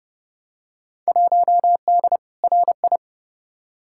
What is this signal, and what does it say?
Morse code, heard as a keyed tone, about 30 wpm, 700 Hz. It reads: 1D RI
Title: Field Day Exchanges - 30wpm